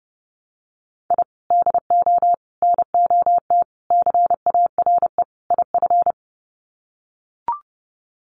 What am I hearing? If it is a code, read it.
Morse code, heard as a keyed tone, about 30 wpm, 700 Hz. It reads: I DO NOT CARE IF
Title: Sets of 5 Words 30wpm